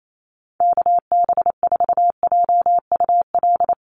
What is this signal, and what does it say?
Morse code, heard as a keyed tone, about 28 wpm, 700 Hz. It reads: KB4JUL
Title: USA Call Signs 28wpm